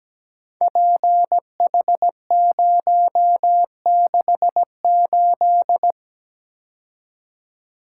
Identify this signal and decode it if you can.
Morse code, heard as a keyed tone, about 17 wpm, 700 Hz. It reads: PH068
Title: Code Groups - 5 Letter:Numbers - 17wpm